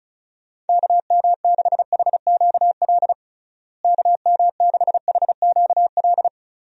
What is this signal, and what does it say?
Morse code, heard as a keyed tone, about 36 wpm, 700 Hz. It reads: KM6HQL KM6HQL
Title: USA Call Signs - 6 Characters - Repeated 3x - 35wpm